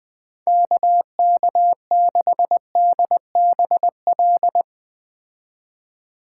Morse code, heard as a keyed tone, about 21 wpm, 700 Hz. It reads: KK6DBL